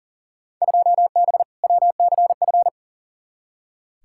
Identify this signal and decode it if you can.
Morse code, heard as a keyed tone, about 40 wpm, 700 Hz. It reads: 2B WCF
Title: Field Day Exchanges Encore - 40wpm